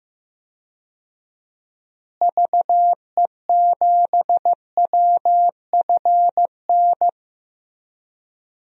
Morse code, heard as a keyed tone, about 15 wpm, 700 Hz. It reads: VE7WFN
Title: DXCC-100 Call Signs - 6 Characters - Repeated 2x - 15wpm